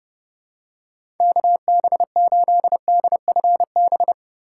Morse code, heard as a keyed tone, about 30 wpm, 700 Hz. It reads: KB8DFB
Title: USA Call Signs - 2x Character Spacing - 30wpm